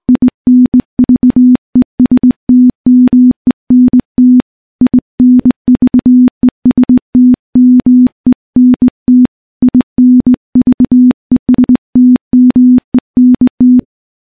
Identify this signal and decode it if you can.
Morse code, heard as a keyed tone, about 20 wpm, 250 Hz. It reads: INVESTMENT INVESTMENT INVESTMENT
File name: morse / morse-training-20WPM